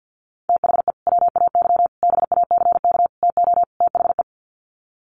Morse code, heard as a keyed tone, about 50 wpm, 700 Hz. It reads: THE WAY BACK TO THE
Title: Sets of 5 Words Encore2 50wpm